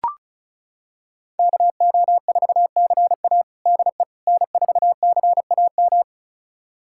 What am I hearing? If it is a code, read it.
Morse code, heard as a keyed tone, about 34 wpm, 700 Hz. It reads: KO4CA DE N4CAM
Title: USA Callsign DE Callsign - 5 Characters - 35wpm